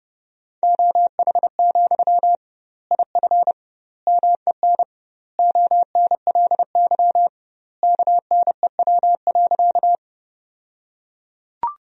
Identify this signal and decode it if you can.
Morse code, heard as a keyed tone, about 32 wpm, 700 Hz. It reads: OH, IF MEN ONLY KNEW.
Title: 2x Word-Spacing - Sentences from Top 500 Words - 30wpm